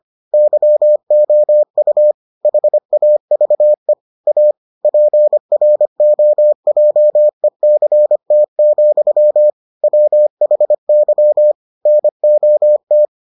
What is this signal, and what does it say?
Morse code, heard as a keyed tone, about 25 wpm, 600 Hz. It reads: YOU HAVE A PROJECT, WHY NOT